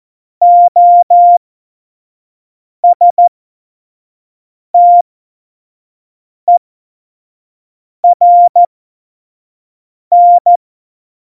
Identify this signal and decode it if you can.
Morse code, heard as a keyed tone, about 14 wpm, spaced as for 5 wpm, 700 Hz. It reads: OSTERN
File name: Morsekurs / Witze / Witz 21 April 2020, 25 F70